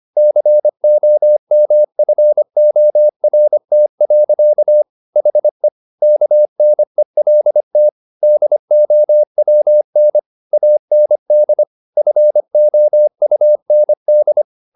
Morse code, heard as a keyed tone, about 25 wpm, 600 Hz. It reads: COMFORT. HE KNELT DOWN AND FOUND
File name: the old man and the sea part 1 25wpm